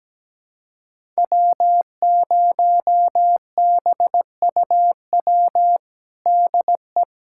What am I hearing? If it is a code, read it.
Morse code, heard as a keyed tone, about 17 wpm, 700 Hz. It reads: W0BUW DE
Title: USA Callsign DE Callsign - 5 Characters - 17wpm